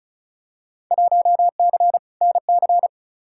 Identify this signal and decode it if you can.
Morse code, heard as a keyed tone, about 36 wpm, 700 Hz. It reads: 1C NC
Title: Field Day Exchanges - 35wpm